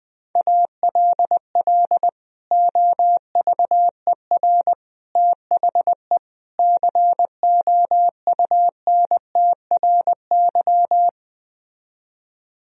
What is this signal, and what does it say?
Morse code, heard as a keyed tone, about 20 wpm, 700 Hz. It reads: ALL OVER THE COUNTRY